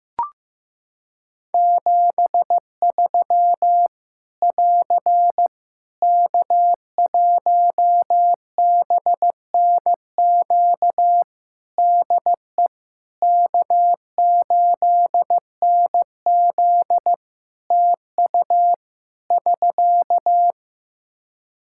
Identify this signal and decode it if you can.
Morse code, heard as a keyed tone, about 15 wpm, 700 Hz. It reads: 73 <AR> K1BNQ DE K8NZ TU <SK>
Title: Practice QSO - Exchange 4 - Ending - 15wpm